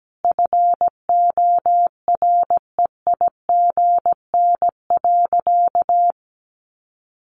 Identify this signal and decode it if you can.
Morse code, heard as a keyed tone, about 17 wpm, 700 Hz. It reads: FOREIGN.
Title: Sentences from Top 1400 Words 17wpm